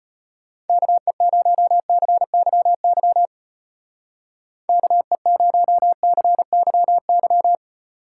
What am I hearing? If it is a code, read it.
Morse code, heard as a keyed tone, about 33 wpm, 700 Hz. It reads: KE0CYY KE0CYY